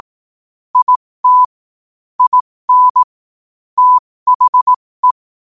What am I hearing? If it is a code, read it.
Morse code, heard as a keyed tone, about 18 wpm, spaced as for 15 wpm, 1000 Hz. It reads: IT IN THE
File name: qotdfiles / 15wpm